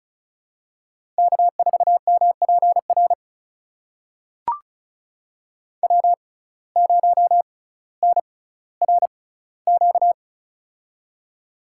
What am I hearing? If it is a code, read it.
Morse code, heard as a keyed tone, about 36 wpm, spaced as for 18 wpm, 700 Hz. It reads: K4MPR W0NRQ
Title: USA Call Signs - 5 Characters - 6x Character Spacing - 35wpm